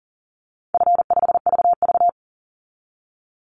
Morse code, heard as a keyed tone, about 41 wpm, 700 Hz. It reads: F5VV
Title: Call Signs - 40wpm